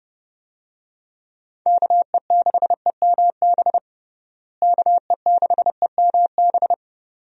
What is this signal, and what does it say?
Morse code, heard as a keyed tone, about 31 wpm, 700 Hz. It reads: KE6EMB KE6EMB
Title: USA Call Signs - 6 Characters - Repeated 2x - 30wpm